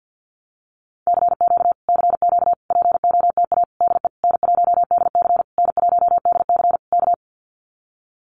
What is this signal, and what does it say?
Morse code, heard as a keyed tone, about 50 wpm, 700 Hz. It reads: CQ CQ POTA DE N1DC N1DC K
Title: Common QSO Phrases Encore 50wpm